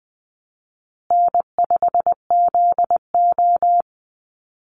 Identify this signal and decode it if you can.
Morse code, heard as a keyed tone, about 20 wpm, 700 Hz. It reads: N5ZO
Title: Easy Call Signs Repeated 2x - 20wpm